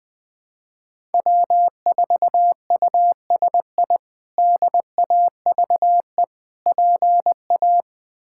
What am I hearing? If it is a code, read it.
Morse code, heard as a keyed tone, about 20 wpm, 700 Hz. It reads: W4USI DAVE PA
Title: K1USN SST QSO Practice - 20wpm